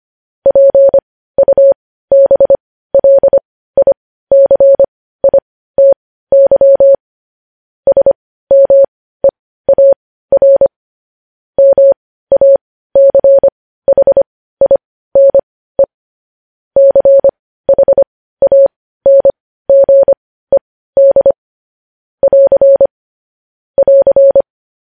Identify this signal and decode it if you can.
Morse code, heard as a keyed tone, about 25 wpm, spaced as for 15 wpm, 550 Hz. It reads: PUBLICITY SMEAR MACHINE CHANGED <AR> <AR>
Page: News headlines in Morse code at 15 wpm